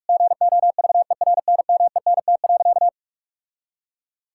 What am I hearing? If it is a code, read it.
Morse code, heard as a keyed tone, about 46 wpm, 700 Hz. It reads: GOVERNMENT.
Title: Sentences from Top 1000 Words - Review Part 6 - 45wpm